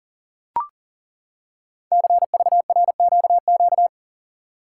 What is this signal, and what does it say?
Morse code, heard as a keyed tone, about 42 wpm, 700 Hz. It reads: CVRQQ
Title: Code Groups - 5 Letters - 40wpm